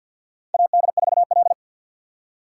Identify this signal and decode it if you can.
Morse code, heard as a keyed tone, about 51 wpm, 690 Hz. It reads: AD4L
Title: USA Call Signs - 4 Characters - 6x Character Spacing - 50wpm